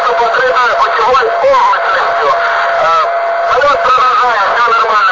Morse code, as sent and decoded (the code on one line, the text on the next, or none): none